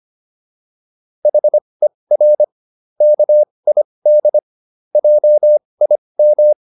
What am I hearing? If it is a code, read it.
Morse code, heard as a keyed tone, about 25 wpm, spaced as for 20 wpm, 600 Hz. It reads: HER KID JIM